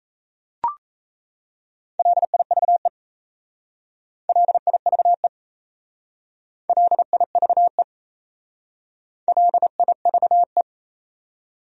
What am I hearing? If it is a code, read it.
Morse code, heard as a keyed tone, about 33 wpm, 700 Hz. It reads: LIVE LIVE LIVE LIVE
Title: Speed-Racing Top 300 Words 28wpm